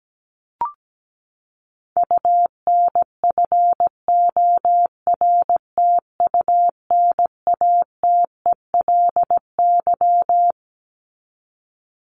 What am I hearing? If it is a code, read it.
Morse code, heard as a keyed tone, about 17 wpm, 700 Hz. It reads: UNFORTUNATELY
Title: Top 1751-2000 Words 17wpm